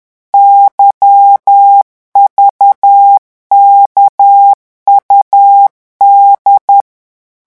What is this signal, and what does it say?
Morse code, heard as a keyed tone, about 10 wpm, 800 Hz. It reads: YVKUD
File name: morse mp3 / revision3-vitesse 10 mots